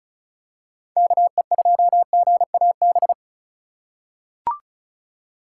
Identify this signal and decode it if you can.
Morse code, heard as a keyed tone, about 36 wpm, 700 Hz. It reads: KE2GAB
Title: USA Call Signs - 6 Characters - 4x Character Spacing - 35wpm